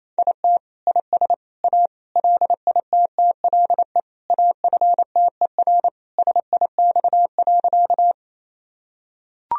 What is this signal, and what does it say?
Morse code, heard as a keyed tone, about 27 wpm, 700 Hz. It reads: IT IS A LITTLE AFTER SIX.
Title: Sentences from Top 1000 Words - Review Part 2 - 28wpm